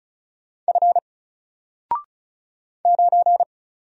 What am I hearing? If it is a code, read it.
Morse code, heard as a keyed tone, about 35 wpm, 700 Hz. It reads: F 9
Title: Single Character (Letter:Number:Punctuation:Prosign) 35wpm